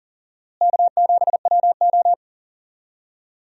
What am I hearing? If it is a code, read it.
Morse code, heard as a keyed tone, about 40 wpm, 700 Hz. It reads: K7WO
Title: USA Call Signs - 4 Characters - Repeated 2x - 40wpm